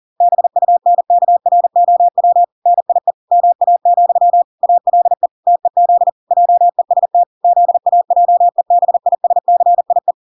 Morse code, heard as a keyed tone, about 40 wpm, 700 Hz. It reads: BUNKROW NIE MA, ALE TEZ JEST ZAJEBISCIE